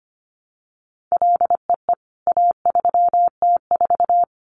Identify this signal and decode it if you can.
Morse code, heard as a keyed tone, about 25 wpm, 700 Hz. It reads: LEE A3T4
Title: CWT Exchange 25wpm